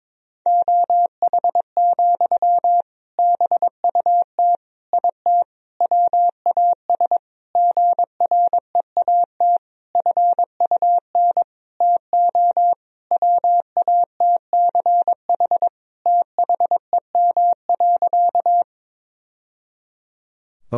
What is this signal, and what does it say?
Morse code, heard as a keyed tone, about 22 wpm, 700 Hz. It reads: OH, BUT IT WAS GREAT FUN TO WATCH THEM.
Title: Sentences from Top 1000 Words - Review Part 4 - 22wpm